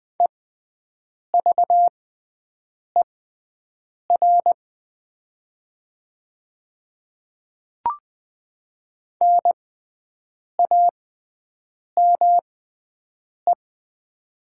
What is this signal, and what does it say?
Morse code, heard as a keyed tone, about 20 wpm, spaced as for 7 wpm, 700 Hz. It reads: EVER NAME
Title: 6x Character Spacing - Top 500 Words - 20wpm